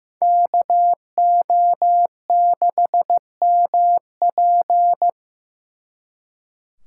Morse code, heard as a keyed tone, about 18 wpm, 700 Hz. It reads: KO6MP
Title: Speed Racing - USA 2x2 Call Signs 15wpm